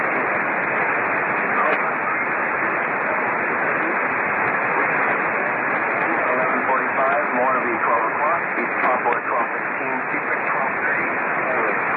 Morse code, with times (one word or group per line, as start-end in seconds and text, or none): none